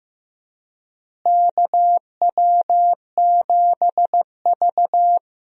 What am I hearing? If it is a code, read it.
Morse code, heard as a keyed tone, about 15 wpm, 700 Hz. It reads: KW7V
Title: USA Call Signs - 4 Characters - Repeated 2x - 15wpm